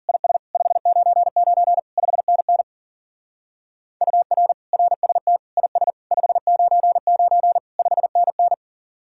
Morse code, heard as a keyed tone, about 47 wpm, 700 Hz. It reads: IS 599 5NN UR RST IS 599 5NN
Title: Speed-Racing Common QSO Phrases - 40wpm